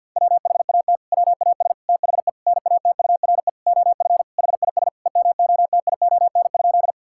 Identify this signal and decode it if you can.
Morse code, heard as a keyed tone, about 50 wpm, 700 Hz. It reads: WHAT WAS THE NATURE OF HIS EMOTION?
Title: Sentences from Top 1600 Words 50wpm